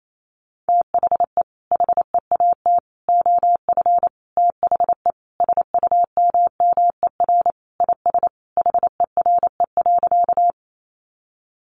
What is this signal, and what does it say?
Morse code, heard as a keyed tone, about 28 wpm, 700 Hz. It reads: THE HEAT OF THE SUMMER IS HERE.
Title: Sentences from Top 1100 Words 28wpm